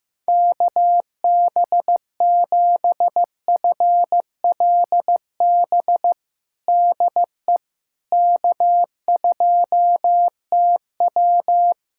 Morse code, heard as a keyed tone, about 15 wpm, 700 Hz. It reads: KB7FLB DE K2TW
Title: Practice QSO - Exchange 4 - Ending - 15wpm